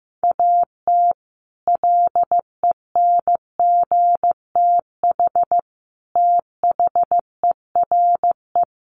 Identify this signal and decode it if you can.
Morse code, heard as a keyed tone, about 15 wpm, 700 Hz. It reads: AT LENGTH THERE
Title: Sentences from Top 1000 Words 15wpm